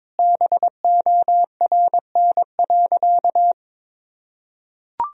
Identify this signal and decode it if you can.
Morse code, heard as a keyed tone, about 22 wpm, 700 Hz. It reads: BORN.
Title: Sentences from Top 1000 Words - Review Part 2 - 22wpm